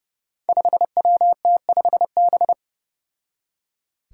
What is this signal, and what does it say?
Morse code, heard as a keyed tone, about 30 wpm, 700 Hz. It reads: 5WT5B
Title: Five Challenging Letter-Number Combinations 30wpm